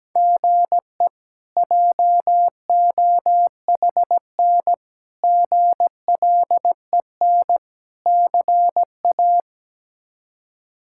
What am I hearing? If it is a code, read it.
Morse code, heard as a keyed tone, about 17 wpm, 700 Hz. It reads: GE JOHN GLEN CA
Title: K1USN SST QSO Practice - 17wpm